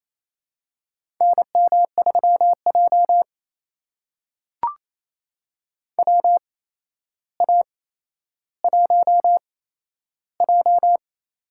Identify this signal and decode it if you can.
Morse code, heard as a keyed tone, about 27 wpm, spaced as for 12 wpm, 700 Hz. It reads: NM3J WA1J
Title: USA Call Signs - 4 Characters - 8x Character Spacing - 28wpm